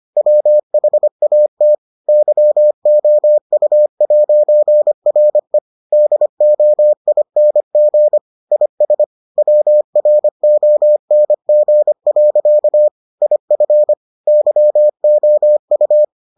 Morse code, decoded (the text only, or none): WHAT YOU'RE DOING IS WRONG. IF YOU